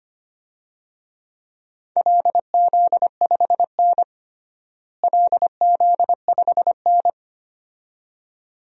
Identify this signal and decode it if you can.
Morse code, heard as a keyed tone, about 25 wpm, 700 Hz. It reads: LZ5N LZ5N
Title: DXCC-100 Call Signs - 4 Characters - Repeated 2x - 25wpm